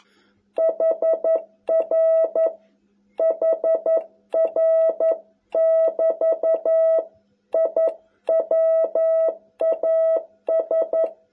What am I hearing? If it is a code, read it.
Morse code, heard as a keyed tone, about 11 wpm, 660 Hz. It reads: HR HR <BT> I WAS